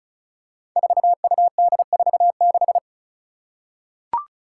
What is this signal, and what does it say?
Morse code, heard as a keyed tone, about 34 wpm, 700 Hz. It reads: 4UD46
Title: Five Challenging Letter-Number Combinations 35wpm